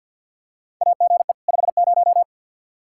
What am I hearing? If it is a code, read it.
Morse code, heard as a keyed tone, about 47 wpm, 700 Hz. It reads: AGE 50
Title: Practice QSO - Exchange 3 - More Info - 50wpm